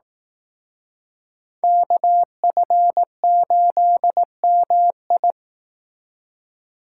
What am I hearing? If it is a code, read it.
Morse code, heard as a keyed tone, about 18 wpm, 700 Hz. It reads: KF8MI
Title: Speed Racing - USA 2x2 Call Signs 15wpm